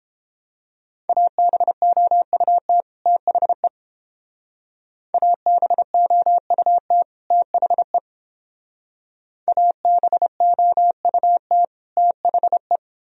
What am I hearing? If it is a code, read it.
Morse code, heard as a keyed tone, about 28 wpm, 700 Hz. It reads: ABOUT THE ABOUT THE ABOUT THE